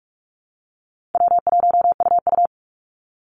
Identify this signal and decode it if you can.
Morse code, heard as a keyed tone, about 44 wpm, 700 Hz. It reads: W1UU
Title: Call Signs Repeated 2x - 45wpm